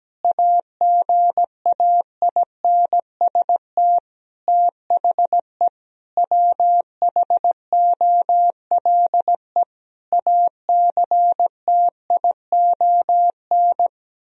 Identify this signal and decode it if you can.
Morse code, heard as a keyed tone, about 17 wpm, 700 Hz. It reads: AGAINST THE WHOLE ACTION
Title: Sentences from Top 1000 Words - Review Part 4 - 17wpm